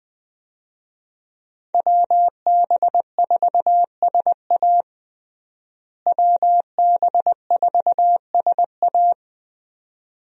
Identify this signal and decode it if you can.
Morse code, heard as a keyed tone, about 20 wpm, 700 Hz. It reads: WB4SA WB4SA
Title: USA Call Signs - 5 Characters - Repeated 2x - 20wpm